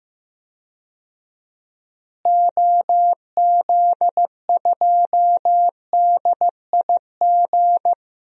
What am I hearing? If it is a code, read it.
Morse code, heard as a keyed tone, about 15 wpm, 700 Hz. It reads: OZ2DIG
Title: DXCC-100 Call Signs - 6 Characters - Repeated 2x - 15wpm